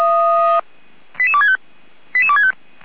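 Five-tone selective call (ZVEI), heard as quick runs of five tones, tones of about 70 ms. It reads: 88265 88265
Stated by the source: ZVEI five-tone call received by SDR.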